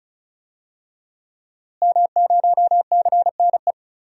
Morse code, heard as a keyed tone, about 36 wpm, 700 Hz. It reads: M0CNE
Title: DXCC-100 Call Signs - 5 Characters - Repeated 3x - 35wpm